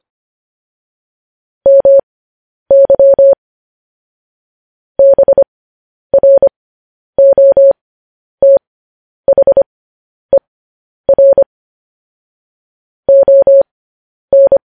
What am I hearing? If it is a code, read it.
Morse code, heard as a keyed tone, about 25 wpm, spaced as for 10 wpm, 550 Hz. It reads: MY BROTHER ON